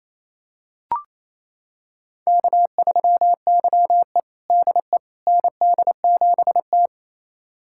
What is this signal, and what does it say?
Morse code, heard as a keyed tone, about 27 wpm, 700 Hz. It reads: K3YE DE ND7T
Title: USA Callsign DE Callsign - 4 Characters - 28wpm